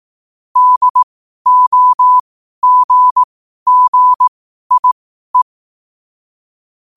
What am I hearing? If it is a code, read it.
Morse code, heard as a keyed tone, about 18 wpm, spaced as for 12 wpm, 1000 Hz. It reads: DOGGIE